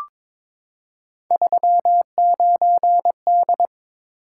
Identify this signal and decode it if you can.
Morse code, heard as a keyed tone, about 22 wpm, 700 Hz. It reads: 39D
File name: practice / 22wpm / Three Letter-Number Combinations - 22wpm